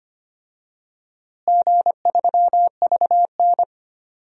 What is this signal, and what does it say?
Morse code, heard as a keyed tone, about 25 wpm, 700 Hz. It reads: G3VN
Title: DXCC-100 Call Signs - 4 Characters - Repeated 2x - 25wpm